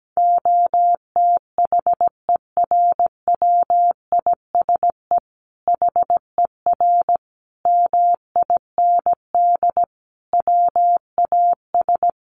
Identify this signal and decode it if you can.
Morse code, heard as a keyed tone, about 17 wpm, 700 Hz. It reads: OTHERWISE HER MIND WAS
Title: Sentences from Top 1200 Words 17wpm